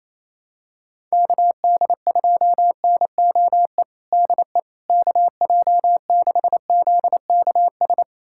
Morse code, heard as a keyed tone, about 28 wpm, 700 Hz. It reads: KD2NOE DE KJ6ZKS
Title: USA Callsign DE Callsign - 6 Characters - 28wpm